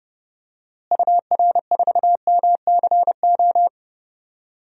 Morse code, heard as a keyed tone, about 30 wpm, 700 Hz. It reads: UR4MCO
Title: DXCC-100 Call Signs - 6 Characters - Repeated 2x - 30wpm